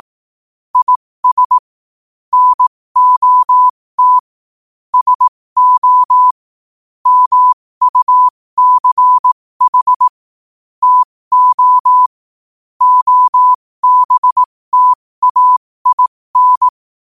Morse code, heard as a keyed tone, about 18 wpm, spaced as for 15 wpm, 1000 Hz. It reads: IS NOT SO MUCH TO OBTAIN